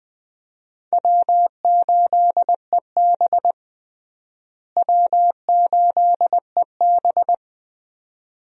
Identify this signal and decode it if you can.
Morse code, heard as a keyed tone, about 21 wpm, 700 Hz. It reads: W8EB W8EB
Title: USA Call Signs - 4 Characters - Repeated 2x - 20wpm